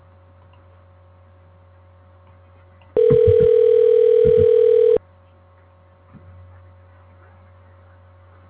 Ringback tone (North America), 440 + 480 Hz.